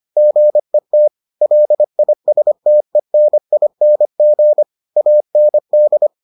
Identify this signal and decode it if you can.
Morse code, heard as a keyed tone, about 25 wpm, 600 Hz. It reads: GET LISTENING AND